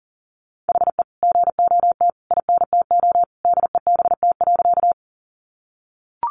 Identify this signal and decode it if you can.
Morse code, heard as a keyed tone, about 40 wpm, 700 Hz. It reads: HE GOT INTO DEBT.
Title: Sentences from Top 1300 Words 40wpm